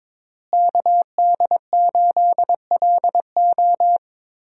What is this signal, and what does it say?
Morse code, heard as a keyed tone, about 22 wpm, 700 Hz. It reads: KD8LO
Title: USA Call Signs - 5 Characters - Repeated 3x - 22wpm